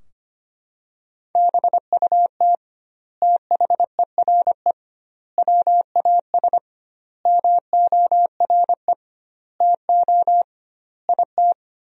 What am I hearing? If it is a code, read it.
Morse code, heard as a keyed tone, about 25 wpm, 700 Hz. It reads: BUT THERE WAS MORE TO IT
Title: 2x Word-Spacing - Sentences from Top 100 Words - 25wpm